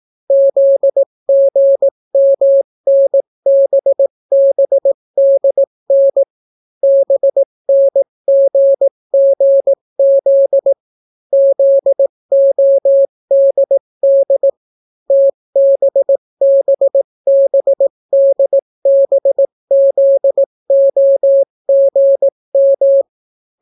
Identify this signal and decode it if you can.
Morse code, heard as a keyed tone, about 18 wpm, 550 Hz. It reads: ZGMNBBDN BNGGZ ZODD TBBBDBZOGM